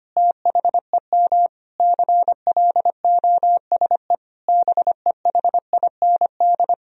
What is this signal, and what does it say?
Morse code, heard as a keyed tone, about 25 wpm, 700 Hz. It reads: THEM CLOSE BEHIND
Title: Sentences from Top 1000 Words - Review Part 1 - 25wpm